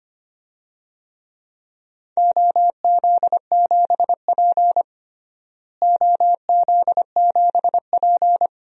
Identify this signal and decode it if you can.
Morse code, heard as a keyed tone, about 25 wpm, 700 Hz. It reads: OZ7P OZ7P
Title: DXCC-100 Call Signs - 4 Characters - Repeated 2x - 25wpm